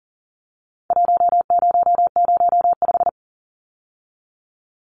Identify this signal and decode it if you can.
Morse code, heard as a keyed tone, about 40 wpm, 700 Hz. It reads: 1005
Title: Numbers - Contest Style - 4-Digit 40wpm